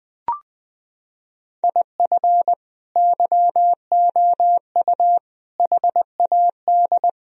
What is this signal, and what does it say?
Morse code, heard as a keyed tone, about 20 wpm, 700 Hz. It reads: IF YOU HAD
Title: Sentences from Top 1000 Words - Review Part 1 - 20wpm